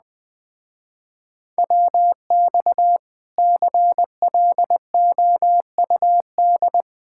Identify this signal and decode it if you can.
Morse code, heard as a keyed tone, about 20 wpm, 700 Hz. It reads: WX CLOUD
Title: Speed-Racing Common QSO Phrases - 17wpm